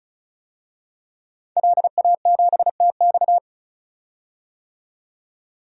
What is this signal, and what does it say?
Morse code, heard as a keyed tone, about 36 wpm, 700 Hz. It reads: LA7TX
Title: DXCC-100 Call Signs - 5 Characters - 35wpm